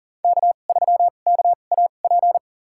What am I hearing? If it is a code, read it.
Morse code, heard as a keyed tone, about 38 wpm, spaced as for 28 wpm, 700 Hz. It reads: K3KAP
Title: USA Call Signs - 2x Character Spacing - 40wpm